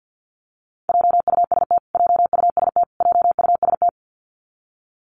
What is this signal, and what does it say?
Morse code, heard as a keyed tone, about 50 wpm, 700 Hz. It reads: JUST JUST JUST
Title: Top 300 Words Repeated 50wpm